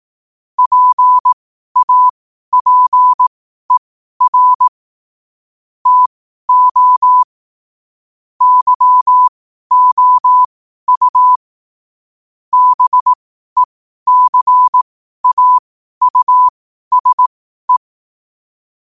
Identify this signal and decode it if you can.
Morse code, heard as a keyed tone, about 18 wpm, spaced as for 12 wpm, 1000 Hz. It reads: PAPER TO YOU BECAUSE
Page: New quotes every day in morse code at 12 Words per minute.